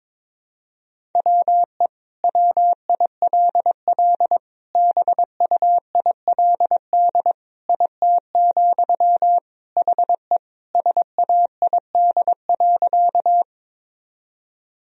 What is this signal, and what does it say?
Morse code, heard as a keyed tone, about 22 wpm, 700 Hz. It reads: WE WILL BUILD IT, HE SAID.
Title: Sentences from Top 1000 Words - Review Part 1 - 22wpm